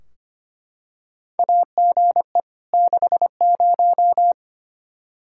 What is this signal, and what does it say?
Morse code, heard as a keyed tone, about 25 wpm, 700 Hz. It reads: AGE 60
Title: Practice QSO - Exchange 3 - More Info - 25wpm